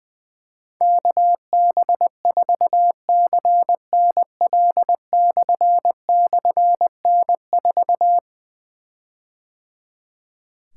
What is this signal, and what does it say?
Morse code, heard as a keyed tone, about 20 wpm, 700 Hz. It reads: KB4CNL//N4
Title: Difficult Call Signs 20wpm